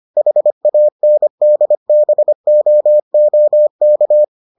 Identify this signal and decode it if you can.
Morse code, heard as a keyed tone, about 25 wpm, 600 Hz. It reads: HANDBOOK